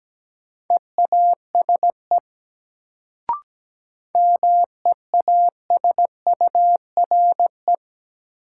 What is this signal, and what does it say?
Morse code, heard as a keyed tone, about 17 wpm, 700 Hz. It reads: EASE MEASURE
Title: Sentences from Top 1000 Words - Review Part 1 - New Word Drill - 17wpm